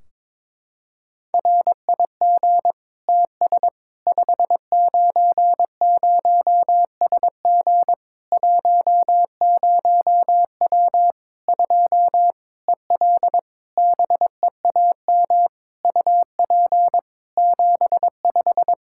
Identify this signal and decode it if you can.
Morse code, heard as a keyed tone, about 22 wpm, 700 Hz. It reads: RIG TS 590SG 10W 2 EL BEAM UP 75
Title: Practice QSO - Exchange 2 - Info - 22wpm